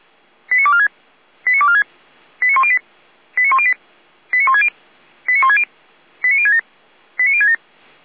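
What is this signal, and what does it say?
Five-tone selective call (ZVEI), heard as quick runs of five tones, tones of about 70 ms. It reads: 89246 89246 89108 89108 89160 89160 89067 89067
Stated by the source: ZVEI five-tone selective call 89560 received with an SDR.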